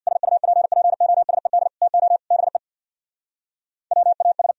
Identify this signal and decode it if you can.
Morse code, heard as a keyed tone, about 63 wpm, 700 Hz. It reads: SUPPOSED TO BE WAS